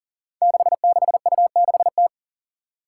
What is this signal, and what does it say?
Morse code, heard as a keyed tone, about 40 wpm, 700 Hz. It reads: 66U6T
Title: Five Challenging Letter-Number Combinations 40wpm